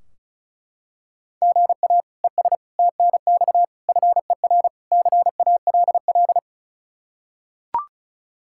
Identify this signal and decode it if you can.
Morse code, heard as a keyed tone, about 36 wpm, 700 Hz. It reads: GA ES TNX FER CALL
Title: Practice QSO - Exchange 1 - Beginning - 35wpm